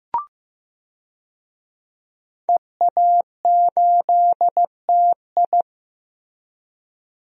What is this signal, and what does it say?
Morse code, heard as a keyed tone, about 15 wpm, 700 Hz. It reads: EA8TI
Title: DXCC-100 Call Signs - 5 Characters - Repeated 2x - 15wpm